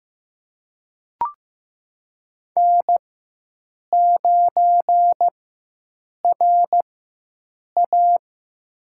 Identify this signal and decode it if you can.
Morse code, heard as a keyed tone, about 15 wpm, spaced as for 7 wpm, 700 Hz. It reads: N9RA
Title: USA Call Signs - 4 Characters - 4x Character Spacing - 15wpm